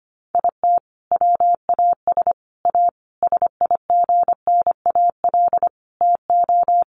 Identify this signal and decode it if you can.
Morse code, heard as a keyed tone, about 25 wpm, 700 Hz. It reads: IT WAS A SIGNAL TO
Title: Sentences from Top 1400 Words 25wpm